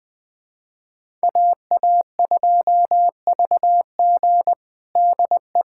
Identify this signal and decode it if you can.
Morse code, heard as a keyed tone, about 21 wpm, 700 Hz. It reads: AA2VG DE
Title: USA Callsign DE Callsign - 5 Characters - 20wpm